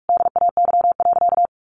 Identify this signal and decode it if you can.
Morse code, heard as a keyed tone, about 44 wpm, 700 Hz. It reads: DAY.
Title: Sentences from Top 300 Words 45wpm